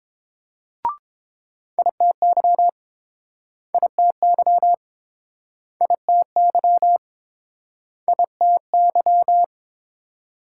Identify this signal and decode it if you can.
Morse code, heard as a keyed tone, about 29 wpm, 700 Hz. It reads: ITY ITY ITY ITY